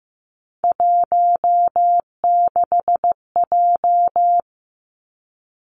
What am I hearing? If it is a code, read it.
Morse code, heard as a keyed tone, about 15 wpm, 700 Hz. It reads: 16J
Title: Three Challenging Letter-Number Combinations 15wpm